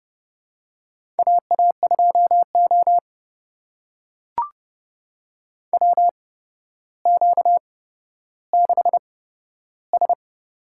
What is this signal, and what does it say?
Morse code, heard as a keyed tone, about 30 wpm, spaced as for 13 wpm, 700 Hz. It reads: AA2O WQ6S
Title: USA Call Signs - 4 Characters - 8x Character Spacing - 30wpm